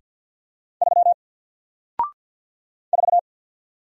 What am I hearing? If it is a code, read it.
Morse code, heard as a keyed tone, about 48 wpm, 700 Hz. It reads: 3 4
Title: Single Character (Letter:Number:Punctuation:Prosign) Alt 50wpm